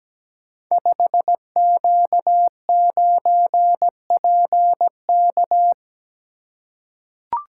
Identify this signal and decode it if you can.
Morse code, heard as a keyed tone, about 17 wpm, 700 Hz. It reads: 5Q9PK
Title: Code Groups - 5 Letter:Numbers - 17wpm